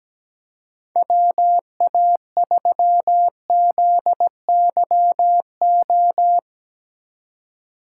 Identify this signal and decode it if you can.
Morse code, heard as a keyed tone, about 17 wpm, 700 Hz. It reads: WA3ZYO
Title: USA Call Signs - 6 Characters - 3x Character Spacing - 17wpm